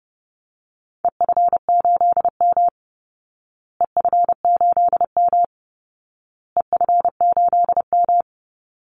Morse code, heard as a keyed tone, about 30 wpm, 700 Hz. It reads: EF8M EF8M EF8M
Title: Call Signs Repeated 2x - 30wpm